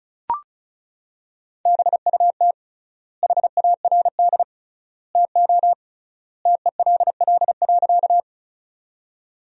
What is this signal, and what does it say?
Morse code, heard as a keyed tone, about 34 wpm, 700 Hz. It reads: BUT HARD TO TELL.
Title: 3x Word-Spacing - Sentences from Top 300 Words - 35wpm